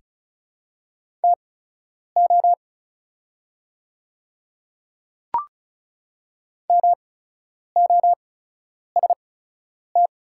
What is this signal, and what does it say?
Morse code, heard as a keyed tone, about 37 wpm, spaced as for 10 wpm, 690 Hz. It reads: TO MOST
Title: 8x Character Spacing - Top 100 Words - 35wpm